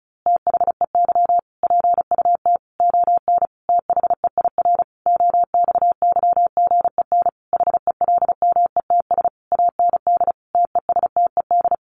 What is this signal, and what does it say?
Morse code, heard as a keyed tone, about 34 wpm, 700 Hz. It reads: THEY PUT ON THEIR OXYGEN HELMETS AND TESTED
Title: A Sound Of Thunder 35wpm